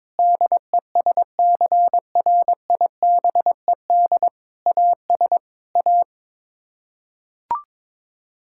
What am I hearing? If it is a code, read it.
Morse code, heard as a keyed tone, about 21 wpm, 700 Hz. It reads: DESCRIBED AS A
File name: practice / 22wpm / Sets of 7 Words - 22wpm